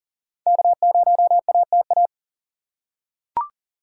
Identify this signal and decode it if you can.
Morse code, heard as a keyed tone, about 40 wpm, 700 Hz. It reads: K0ATA
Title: USA Call Signs - 5 Characters - 4x Character Spacing - 40wpm